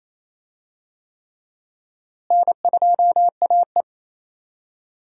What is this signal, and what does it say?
Morse code, heard as a keyed tone, about 27 wpm, 700 Hz. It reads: N2AE
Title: USA Call Signs - 4 Characters - Repeated 3x - 28wpm